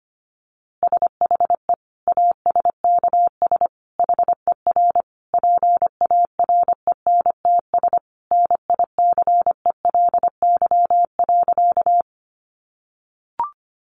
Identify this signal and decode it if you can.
Morse code, heard as a keyed tone, about 25 wpm, 700 Hz. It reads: SHE ASKS HER PARENTS NICELY.
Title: Easy Phrases 25wpm